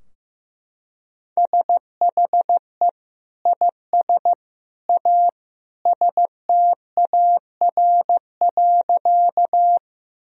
Morse code, heard as a keyed tone, about 15 wpm, 700 Hz. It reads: SHE IS A STAR.
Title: Sentences from Top 1000 Words - Review Part 2 - 15wpm